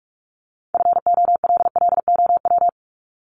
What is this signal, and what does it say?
Morse code, heard as a keyed tone, about 44 wpm, 700 Hz. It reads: FOLLOW
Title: ING Suffix - Top 100 Words with Base Words 45wpm